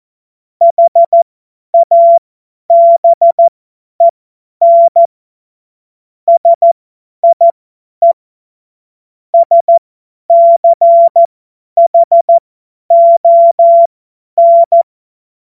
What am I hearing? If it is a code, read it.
Morse code, heard as a keyed tone, about 14 wpm, spaced as for 10 wpm, 680 Hz. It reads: HABEN SIE SCHON
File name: Morsekurs / Witze / Witz 13. Januar 2026 50 F70